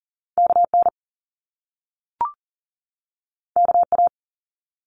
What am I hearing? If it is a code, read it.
Morse code, heard as a keyed tone, about 42 wpm, 700 Hz. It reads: KN KA
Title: Top Two Letter Combinations in Words 40wpm